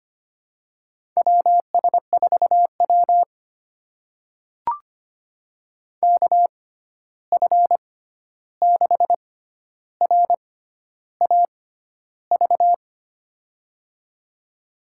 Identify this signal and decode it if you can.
Morse code, heard as a keyed tone, about 25 wpm, spaced as for 11 wpm, 700 Hz. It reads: WS4W KF6RAV
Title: USA Call Signs - 6x Character Spacing - 25wpm